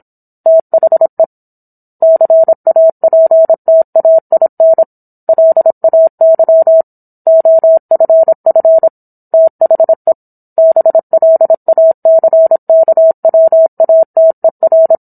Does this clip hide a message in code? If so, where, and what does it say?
Yes: Morse code from 0.5 to 14.9 s, heard as a keyed tone, about 27 wpm, 650 Hz. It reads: THE CAPTAIN LAY OFF THE BLACKWATER